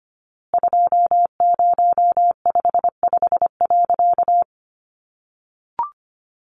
Morse code, heard as a keyed tone, about 25 wpm, 700 Hz. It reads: 2055.
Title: A Sound Of Thunder 25wpm